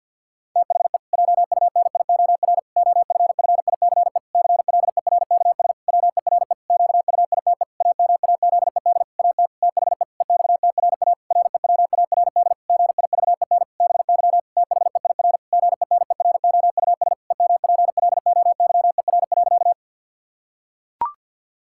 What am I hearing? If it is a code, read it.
Morse code, heard as a keyed tone, about 51 wpm, 700 Hz. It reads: THE JUNIOR OFFICE CLERKS WERE QUITE AMAZED AT THE EXTRA REWARD GIVEN BY THEIR GENEROUS EMPLOYER.
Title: Pangram Challenge v2 - 50wpm